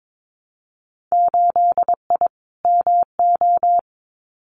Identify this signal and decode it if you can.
Morse code, heard as a keyed tone, about 22 wpm, 700 Hz. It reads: 8I MO
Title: Winter Field Day Exchanges Encore 22wpm